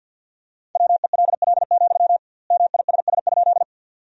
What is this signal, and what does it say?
Morse code, heard as a keyed tone, about 52 wpm, 700 Hz. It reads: WELL, MISS?
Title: 2x Word-Spacing - Sentences from Top 500 Words - 50wpm